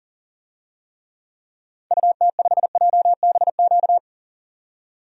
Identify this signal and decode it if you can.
Morse code, heard as a keyed tone, about 40 wpm, 700 Hz. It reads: UT5JBQ
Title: DXCC-100 Call Signs - 6 Characters - 40wpm